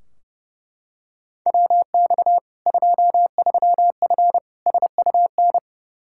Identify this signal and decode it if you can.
Morse code, heard as a keyed tone, about 32 wpm, 700 Hz. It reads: WX 23F SUN
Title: Practice QSO - Exchange 2 - Info - 30wpm